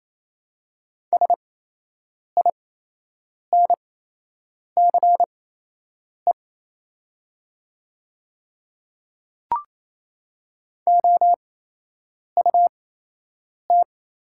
Morse code, heard as a keyed tone, about 28 wpm, spaced as for 8 wpm, 690 Hz. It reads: SINCE OUT
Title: 8x Character Spacing - Top 300 Words - 28wpm